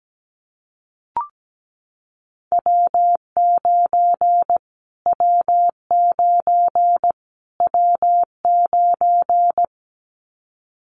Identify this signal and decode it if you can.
Morse code, heard as a keyed tone, about 17 wpm, 700 Hz. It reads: W9 W9 W9
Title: Callsign Prefixes for Top 25 DXCC - Repeated 3x - 17wpm